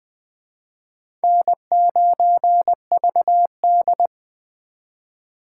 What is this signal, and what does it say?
Morse code, heard as a keyed tone, about 25 wpm, 700 Hz. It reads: N9VD
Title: Speed Racing - USA Call Signs 20wpm